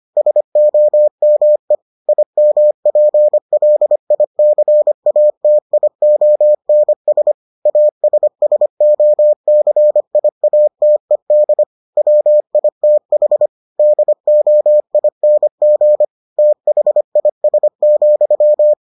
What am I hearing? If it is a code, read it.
Morse code, heard as a keyed tone, about 25 wpm, 600 Hz. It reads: SOME IMPLICATIONS ASSOCIATED WITH DOING THIS,